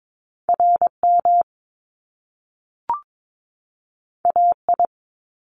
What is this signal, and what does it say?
Morse code, heard as a keyed tone, about 22 wpm, 700 Hz. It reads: RM AI
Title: Top Two Letter Combinations in Words 22wpm